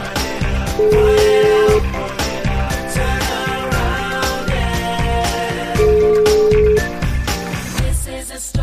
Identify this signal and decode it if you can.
Ringback tone (Europe, 425 Hz), with noise.